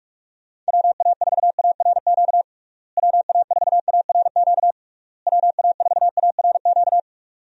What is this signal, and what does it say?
Morse code, heard as a keyed tone, about 41 wpm, 700 Hz. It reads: WA4ARQ WA4ARQ WA4ARQ
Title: USA Call Signs - 6 Characters - Repeated 3x - 45wpm